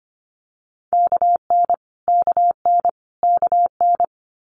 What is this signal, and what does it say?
Morse code, heard as a keyed tone, about 25 wpm, 700 Hz. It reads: KN KN KN
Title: Multi-letter Phonemes Repeated 3x 25wpm